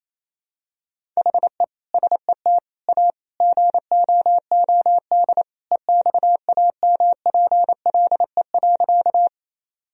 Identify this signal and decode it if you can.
Morse code, heard as a keyed tone, about 28 wpm, 700 Hz. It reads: HE SET A GOOD EXAMPLE.
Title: Sentences from Top 1000 Words - Review Part 1 - 28wpm